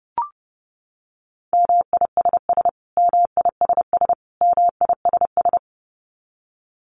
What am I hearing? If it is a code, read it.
Morse code, heard as a keyed tone, about 30 wpm, 700 Hz. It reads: MISS MISS MISS
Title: Top 500 Words Repeated 30wpm